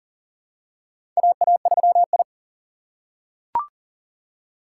Morse code, heard as a keyed tone, about 40 wpm, 700 Hz. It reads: AA3I
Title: USA Call Signs - 4 Characters - 3x Character Spacing - 40wpm